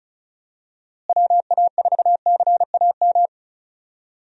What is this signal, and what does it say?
Morse code, heard as a keyed tone, about 36 wpm, 700 Hz. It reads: WA4CAM
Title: USA Call Signs - 6 Characters - Repeated 2x - 35wpm